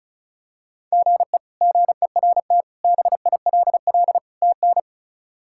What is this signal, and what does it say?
Morse code, heard as a keyed tone, about 34 wpm, 700 Hz. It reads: GE GERT BILL TN
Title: K1USN SST QSO Practice - 35wpm